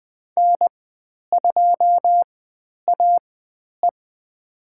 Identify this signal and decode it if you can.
Morse code, heard as a keyed tone, about 20 wpm, spaced as for 10 wpm, 700 Hz. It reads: N2AE
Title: 109 Lesson 18g - Letter m - Call Signs 20wpm